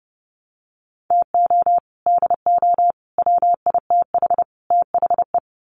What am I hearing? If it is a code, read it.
Morse code, heard as a keyed tone, about 30 wpm, 700 Hz. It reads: TO DO WITH THE
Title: Sets of 4 Words 30wpm